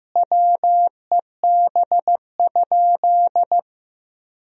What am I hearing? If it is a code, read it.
Morse code, heard as a keyed tone, about 15 wpm, 700 Hz. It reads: WEB?
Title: Sentences from Top 1000 Words - Review Part 4 - 15wpm